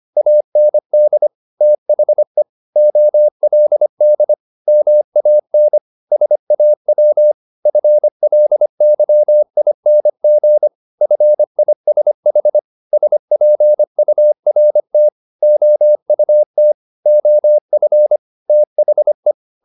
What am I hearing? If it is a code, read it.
Morse code, heard as a keyed tone, about 25 wpm, 600 Hz. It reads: AND THE OLD MAN SAW FLYING FISH SPURT OUT OF THE
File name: the old man and the sea part 1 25wpm